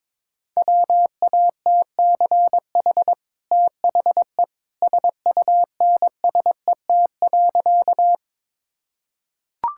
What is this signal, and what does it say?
Morse code, heard as a keyed tone, about 22 wpm, 700 Hz. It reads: WATCH THE SUNSET.
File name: practice / 22wpm / Easy Sentences with Hints - 22wpm